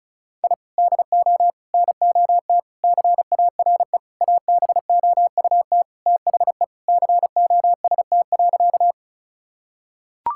Morse code, heard as a keyed tone, about 36 wpm, 700 Hz. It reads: I DO NOT CARE ABOUT THE COST.
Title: Sentences from Top 1000 Words - Review Part 5 - 35wpm